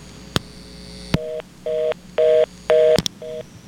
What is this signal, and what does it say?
Reorder tone (North America), 480 + 620 Hz.
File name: VZW CallingRestrictions